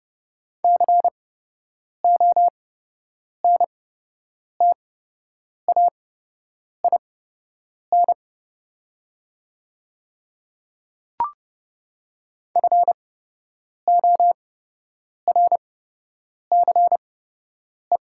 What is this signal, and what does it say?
Morse code, heard as a keyed tone, about 30 wpm, spaced as for 8 wpm, 700 Hz. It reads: CONTAIN FORCE
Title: 8x Character Spacing - Top 500 Words - 30wpm